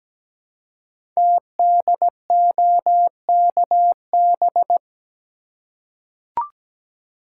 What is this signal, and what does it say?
Morse code, heard as a keyed tone, about 17 wpm, 700 Hz. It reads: TDOKB
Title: Code Groups - 5 Letters - 17wpm